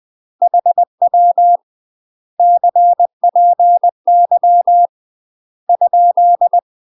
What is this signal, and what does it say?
Morse code, heard as a keyed tone, about 20 wpm, 700 Hz. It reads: HW CPY ?
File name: UK MASTER QSOs TRAINING / 20WPM / UK MASTER QSOs 0037